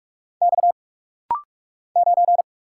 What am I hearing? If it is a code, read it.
Morse code, heard as a keyed tone, about 44 wpm, 700 Hz. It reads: X 9
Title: Single Letter-Number Alt - Rapid-Fire 45wpm